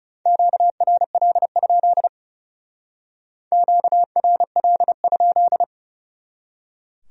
Morse code, heard as a keyed tone, about 37 wpm, 700 Hz. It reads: QRL? QRL?
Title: Speed-Racing Q-Codes - 30wpm